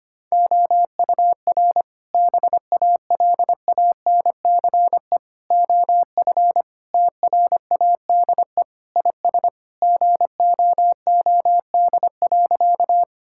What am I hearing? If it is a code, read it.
Morse code, heard as a keyed tone, about 25 wpm, 700 Hz. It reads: OUR BALANCE OF TRADE IS GOOD.
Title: Sentences from Top 1000 Words - Review Part 6 - 25wpm